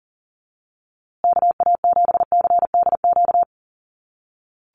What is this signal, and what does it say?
Morse code, heard as a keyed tone, about 40 wpm, 700 Hz. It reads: KA7CDQ
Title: USA Call Signs Repeated 2x 40wpm